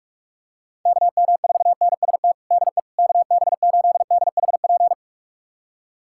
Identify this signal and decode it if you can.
Morse code, heard as a keyed tone, about 46 wpm, 700 Hz. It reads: KM4NST DE KB8DHP
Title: USA Callsign DE Callsign 45wpm